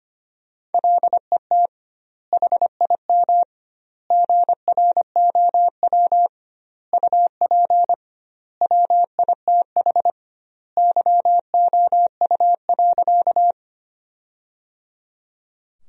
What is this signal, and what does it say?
Morse code, heard as a keyed tone, about 25 wpm, 700 Hz. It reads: LET HIM GROW UP WITH YOU.
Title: 2x Word-Spacing - Sentences from Top 300 Words - 25wpm